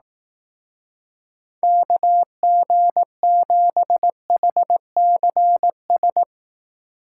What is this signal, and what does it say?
Morse code, heard as a keyed tone, about 18 wpm, 700 Hz. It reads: KG7HCS
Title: Speed Racing - USA 2x3 Call Signs 15wpm